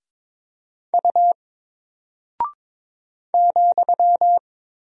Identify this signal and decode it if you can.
Morse code, heard as a keyed tone, about 22 wpm, 700 Hz. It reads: U ,
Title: Single Character (Letter:Number:Punctuation:Prosign) 22wpm